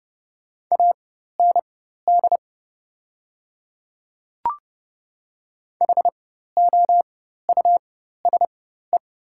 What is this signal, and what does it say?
Morse code, heard as a keyed tone, about 31 wpm, spaced as for 14 wpm, 700 Hz. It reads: AND HOUSE
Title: 4x Character Spacing - Top 300 Words - 30wpm